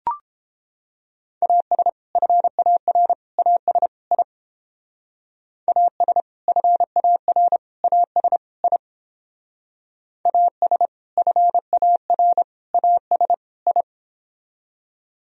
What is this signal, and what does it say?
Morse code, heard as a keyed tone, about 29 wpm, 700 Hz. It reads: AS FAR AS I AS FAR AS I AS FAR AS I